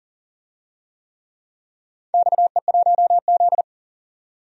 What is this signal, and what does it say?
Morse code, heard as a keyed tone, about 40 wpm, 700 Hz. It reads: XE1Z
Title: DXCC-100 Call Signs - 4 Characters - 40wpm